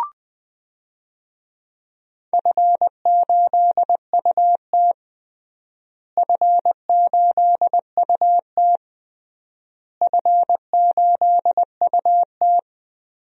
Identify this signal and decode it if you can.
Morse code, heard as a keyed tone, about 20 wpm, 700 Hz. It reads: F8UT F8UT F8UT
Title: DXCC-100 Call Signs - 4 Characters - Repeated 3x - 20wpm